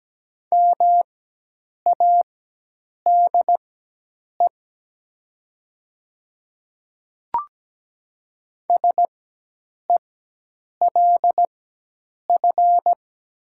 Morse code, heard as a keyed tone, about 17 wpm, spaced as for 8 wpm, 700 Hz. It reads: MADE SELF
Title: 4x Character Spacing - Top 200 Words - 17wpm